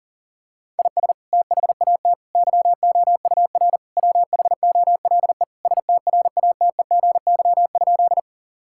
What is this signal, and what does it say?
Morse code, heard as a keyed tone, about 40 wpm, 700 Hz. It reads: IS THAT YOUR WHOLE STRATEGY?
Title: Sentences from Top 1000 Words - Review Part 5 - 40wpm